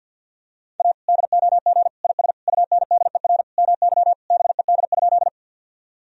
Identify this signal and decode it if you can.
Morse code, heard as a keyed tone, about 51 wpm, 700 Hz. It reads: A DOG IS UNDER MY BED?
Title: Sentences from Top 1000 Words - Review Part 2 - 50wpm